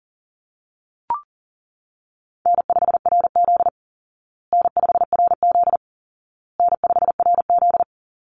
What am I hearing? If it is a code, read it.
Morse code, heard as a keyed tone, about 40 wpm, 700 Hz. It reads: N5RZ N5RZ N5RZ
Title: Easy Call Signs Repeated 3x - 40wpm